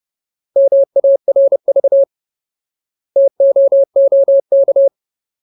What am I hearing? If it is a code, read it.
Morse code, heard as a keyed tone, about 30 wpm, 550 Hz. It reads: MARV TOOK
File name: Ultimate Word List / 30WPM Ultimate